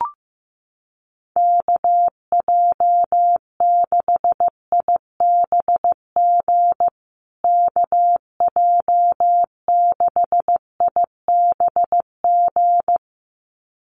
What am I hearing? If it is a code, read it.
Morse code, heard as a keyed tone, about 15 wpm, 700 Hz. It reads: KJ6IBG KJ6IBG
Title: USA Call Signs Repeated 2x 15wpm